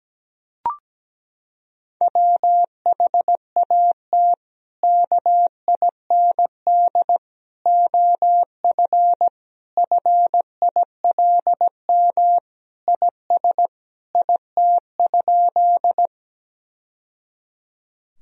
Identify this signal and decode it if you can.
Morse code, heard as a keyed tone, about 17 wpm, 700 Hz. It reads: WHAT KIND OF FILM IS IT?
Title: Sentences from Top 1000 Words - Review Part 3 - 17wpm